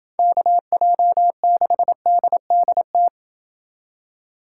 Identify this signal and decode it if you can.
Morse code, heard as a keyed tone, about 27 wpm, 700 Hz. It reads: KJ6DDT